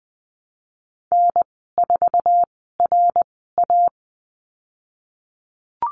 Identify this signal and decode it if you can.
Morse code, heard as a keyed tone, about 20 wpm, spaced as for 15 wpm, 700 Hz. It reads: N4RA
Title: 055 Lesson 10g - Letter r - Call Signs 20wpm